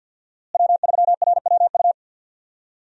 Morse code, heard as a keyed tone, about 50 wpm, 700 Hz. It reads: W3RWU
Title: USA Call Signs - 5 Characters - Repeated 3x - 50wpm